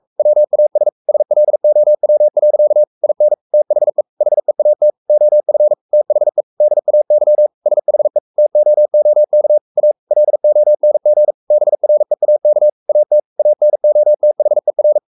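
Morse code, heard as a keyed tone, about 44 wpm, 600 Hz. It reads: WAS SLOW. IN THE HEAT OF THE DAY SHE TOOK A LONG BREAK AT ANOTHER
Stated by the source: Morse code at 43 wpm sending Ken Follett's novel Never in English, from CW med Gnister.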